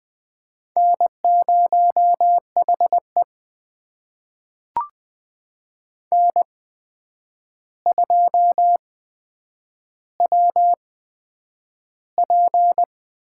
Morse code, heard as a keyed tone, about 20 wpm, spaced as for 9 wpm, 700 Hz. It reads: N0HE N2WP
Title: USA Call Signs - 4 Characters - 8x Character Spacing - 20wpm